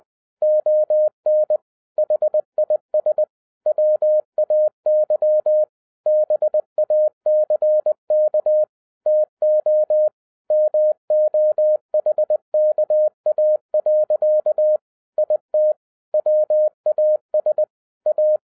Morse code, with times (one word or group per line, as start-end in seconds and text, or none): 0.4-1.6 ON
2.0-3.2 HIS
3.7-5.6 WAY
6.1-8.6 BACK
9.1-10.1 TO
10.5-14.8 MOHKA.
15.2-15.7 IT
16.1-17.6 WAS
18.1-18.4 A